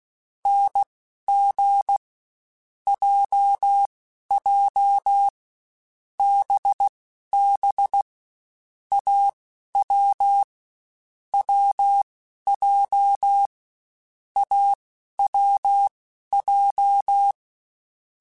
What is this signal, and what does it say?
Morse code, heard as a keyed tone, about 16 wpm, spaced as for 11 wpm, 800 Hz. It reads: NG JJ BB AW WJ AWJ